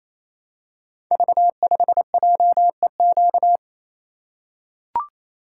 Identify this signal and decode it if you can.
Morse code, heard as a keyed tone, about 27 wpm, 700 Hz. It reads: V5JEQ
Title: Code Groups - 5 Letter:Numbers - 28wpm